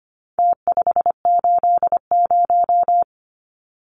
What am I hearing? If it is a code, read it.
Morse code, heard as a keyed tone, about 25 wpm, 700 Hz. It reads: T580
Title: Numbers - Contest Style - 4-Digit 25wpm